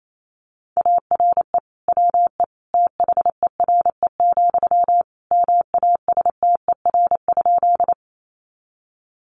Morse code, heard as a keyed tone, about 28 wpm, 700 Hz. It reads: ARE WE THERE, MASTER?
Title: Sentences from Top 1300 Words 28wpm